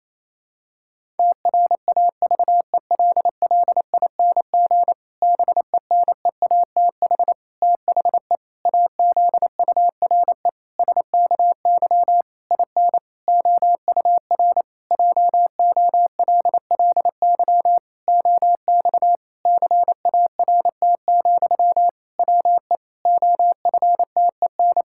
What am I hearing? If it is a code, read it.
Morse code, heard as a keyed tone, about 28 wpm, 700 Hz. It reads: TRAVELLING BENEATH THE AZURE SKY IN OUR JOLLY OX CART, WE OFTEN
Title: Pangram Challenge v2 - 28wpm